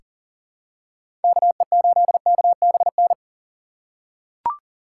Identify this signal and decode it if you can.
Morse code, heard as a keyed tone, about 40 wpm, 700 Hz. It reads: KE8KBN
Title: USA Call Signs - 6 Characters - 6x Character Spacing - 40wpm